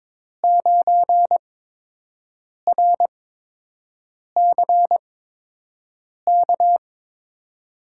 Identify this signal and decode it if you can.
Morse code, heard as a keyed tone, about 21 wpm, spaced as for 6 wpm, 700 Hz. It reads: 9RCK
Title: USA Call Signs - 6 Characters - 8x Character Spacing - 22wpm